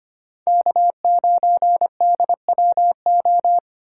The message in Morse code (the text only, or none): K9DWO